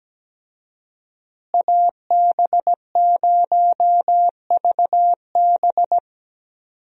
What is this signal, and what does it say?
Morse code, heard as a keyed tone, about 17 wpm, 700 Hz. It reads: AB0VB
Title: USA Call Signs - 5 Characters - Repeated 2x - 17wpm